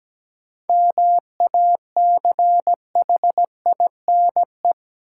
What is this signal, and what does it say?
Morse code, heard as a keyed tone, about 17 wpm, 700 Hz. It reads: MACHINE
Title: Sentences from Top 1000 Words - Review Part 2 - New Word Drill - 17wpm